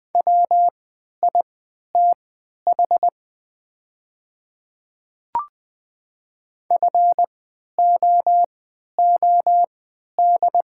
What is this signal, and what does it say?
Morse code, heard as a keyed tone, about 20 wpm, spaced as for 11 wpm, 700 Hz. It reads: WITH FOOD
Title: 3x Character Spacing - Top 300 Words - 20wpm